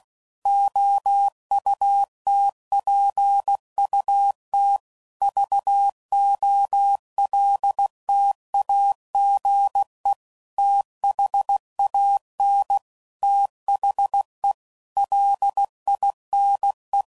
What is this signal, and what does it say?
Morse code, heard as a keyed tone, about 16 wpm, 800 Hz. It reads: OUTPUT VOLTAGE THAN THE LINE
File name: CW Code Groups @15WPM / Technical Subjects / PwrSupply